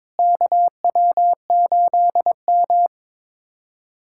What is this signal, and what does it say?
Morse code, heard as a keyed tone, about 22 wpm, 700 Hz. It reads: KW8M
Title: USA Callsign DE Callsign - 4 Characters - 22wpm